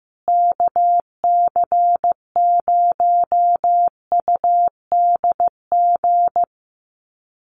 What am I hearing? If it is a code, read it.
Morse code, heard as a keyed tone, about 15 wpm, 700 Hz. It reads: KC0UDG
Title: USA Call Signs Repeated 3x 15wpm